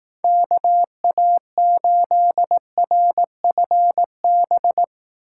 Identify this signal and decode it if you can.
Morse code, heard as a keyed tone, about 18 wpm, 700 Hz. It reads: KA8RFB
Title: Speed Racing - USA 2x3 Call Signs 15wpm